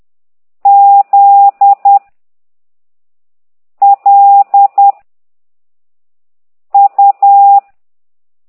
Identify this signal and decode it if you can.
Morse code, heard as a keyed tone, about 10 wpm, 800 Hz. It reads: Z L U